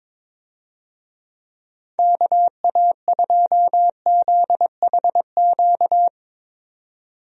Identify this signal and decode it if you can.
Morse code, heard as a keyed tone, about 22 wpm, 700 Hz. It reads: KA2ZHQ
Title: USA Call Signs - 6 Characters - 22wpm